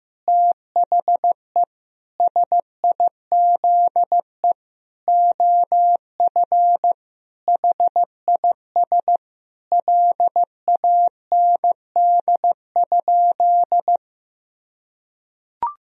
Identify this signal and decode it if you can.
Morse code, heard as a keyed tone, about 15 wpm, 700 Hz. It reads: THE SIZE OF HIS LAND?
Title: Sentences from Top 1000 Words - Review Part 2 - 15wpm